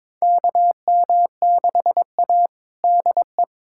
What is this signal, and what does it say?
Morse code, heard as a keyed tone, about 22 wpm, 700 Hz. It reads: KM6A DE
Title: USA Callsign DE Callsign - 4 Characters - 22wpm